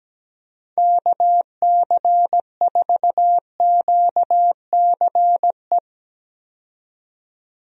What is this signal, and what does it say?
Morse code, heard as a keyed tone, about 17 wpm, 700 Hz. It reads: KC4QCE
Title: USA Call Signs - 6 Characters - 17wpm